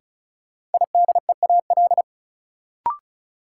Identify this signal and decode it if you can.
Morse code, heard as a keyed tone, about 36 wpm, 700 Hz. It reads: IDEAL
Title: Sentences from Top 1400 Words - New Word Drill - 35wpm